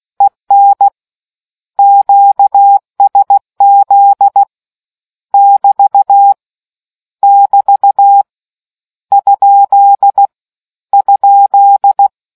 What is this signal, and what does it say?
Morse code, heard as a keyed tone, about 16 wpm, 800 Hz. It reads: EN QSZ <BT> <BT> ? ?